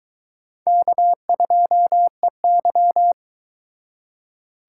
Morse code, heard as a keyed tone, about 22 wpm, 700 Hz. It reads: K2EY